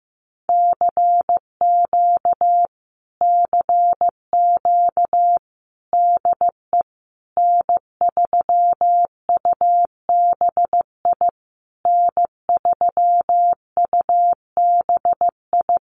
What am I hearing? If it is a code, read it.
Morse code, heard as a keyed tone, about 15 wpm, 700 Hz. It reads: CQ CQ DE N3UBI N3UBI
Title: Common QSO Phrases Encore 15wpm